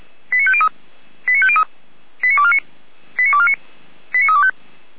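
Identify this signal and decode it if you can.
Five-tone selective call (ZVEI), heard as quick runs of five tones, tones of about 70 ms. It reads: 89503 89503 89249 89249 89325
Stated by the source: ZVEI five-tone selective call received on SDR by the Feuerwehr Orlamünde fire brigade.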